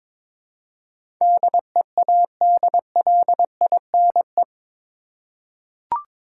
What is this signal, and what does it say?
Morse code, heard as a keyed tone, about 22 wpm, 700 Hz. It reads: DEADLINE
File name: practice / 22wpm / Eight Letter Words - 22wpm